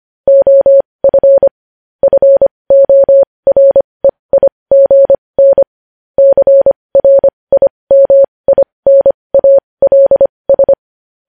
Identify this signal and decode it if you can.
Morse code, heard as a keyed tone, about 25 wpm, spaced as for 20 wpm, 550 Hz. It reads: OF FOREIGN CRIMINALS
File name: podcast / News headlines-1x-25x20